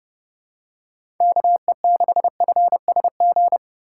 Morse code, heard as a keyed tone, about 30 wpm, 700 Hz. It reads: KE6FSG
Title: USA Call Signs - 6 Characters - 4x Character Spacing - 30wpm